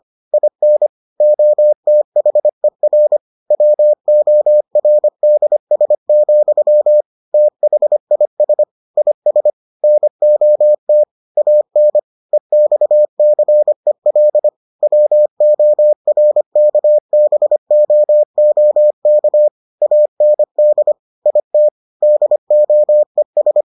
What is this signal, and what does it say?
Morse code, heard as a keyed tone, about 25 wpm, 600 Hz. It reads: IN OTHER WORDS, THIS IS NOT AN EXCEL WORKBOOK AND IT DOES